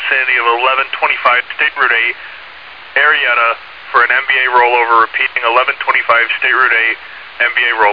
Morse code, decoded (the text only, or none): none